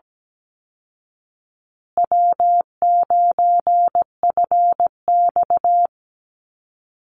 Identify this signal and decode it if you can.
Morse code, heard as a keyed tone, about 17 wpm, 700 Hz. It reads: W9FX
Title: Easy Call Signs Repeated 2x - 17wpm